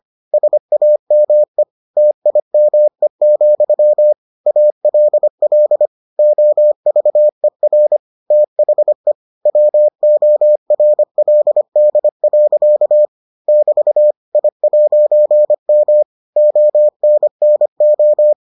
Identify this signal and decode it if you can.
Morse code, heard as a keyed tone, about 25 wpm, 600 Hz. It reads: SAME TIME, ALL OVER THE WORLD. <BT> I'M ONNO